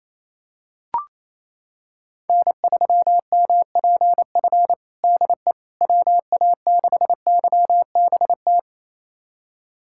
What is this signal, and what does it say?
Morse code, heard as a keyed tone, about 28 wpm, 700 Hz. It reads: N3MPF DE WA6YBT
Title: USA Callsign DE Callsign 28wpm